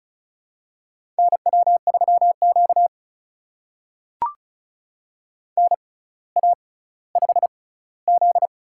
Morse code, heard as a keyed tone, about 36 wpm, spaced as for 18 wpm, 700 Hz. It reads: NW3Q NA5Z
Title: USA Call Signs - 4 Characters - 6x Character Spacing - 35wpm